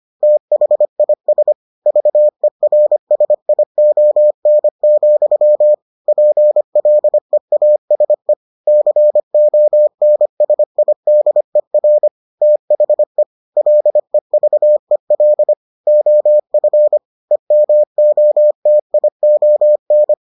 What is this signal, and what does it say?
Morse code, heard as a keyed tone, about 25 wpm, 600 Hz. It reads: THIS VERSION, PLEASE CONSIDER THE LEVEL OF EMOTION